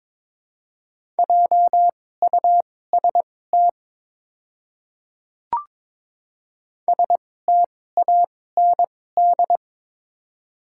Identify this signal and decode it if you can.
Morse code, heard as a keyed tone, about 22 wpm, spaced as for 16 wpm, 700 Hz. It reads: JUST STAND
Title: 2x Character Spacing - Top 300 Words - 22wpm